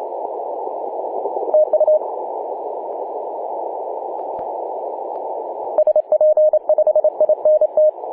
Morse code, heard as a keyed tone, about 27 wpm, 620 Hz, with noise. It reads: TU SP5INT
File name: posts / oct-19-0756z-vk9xsp-24-901mhz-cw